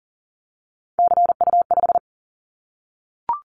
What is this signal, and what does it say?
Morse code, heard as a keyed tone, about 40 wpm, 700 Hz. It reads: CU5
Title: Callsign Prefixes for Top 100 DXCC - 40wpm